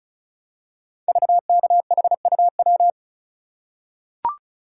Morse code, heard as a keyed tone, about 36 wpm, 700 Hz. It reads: VKHUW
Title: Code Groups - 5 Letters - 35wpm